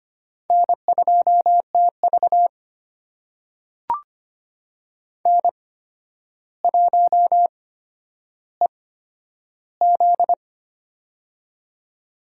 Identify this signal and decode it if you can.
Morse code, heard as a keyed tone, about 25 wpm, spaced as for 11 wpm, 700 Hz. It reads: N2TV N1EZ
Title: USA Call Signs - 8x Character Spacing - 25wpm